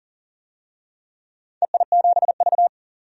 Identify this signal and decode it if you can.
Morse code, heard as a keyed tone, about 39 wpm, 700 Hz. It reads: EI7V
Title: DXCC-100 Call Signs - 4 Characters - 40wpm